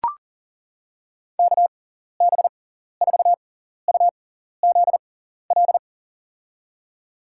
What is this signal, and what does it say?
Morse code, heard as a keyed tone, about 40 wpm, spaced as for 14 wpm, 700 Hz. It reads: KB4UZL
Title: USA Call Signs - 6 Characters - 6x Character Spacing - 40wpm